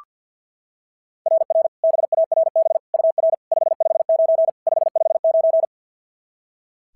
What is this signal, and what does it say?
Morse code, heard as a keyed tone, about 46 wpm, 660 Hz. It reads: RR BARB UR 559 559
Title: SOTA QSO Practice 50wpm